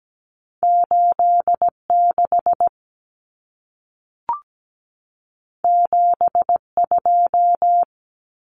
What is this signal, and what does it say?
Morse code, heard as a keyed tone, about 17 wpm, 700 Hz. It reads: 86 72
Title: Common Cut Numbers - 2-Digit 17wpm